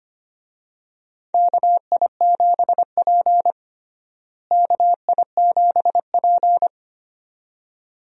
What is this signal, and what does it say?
Morse code, heard as a keyed tone, about 25 wpm, 700 Hz. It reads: KI7P KI7P
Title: USA Call Signs - 4 Characters - Repeated 2x - 25wpm